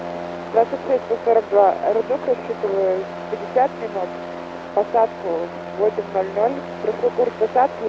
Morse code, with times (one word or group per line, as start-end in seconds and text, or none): none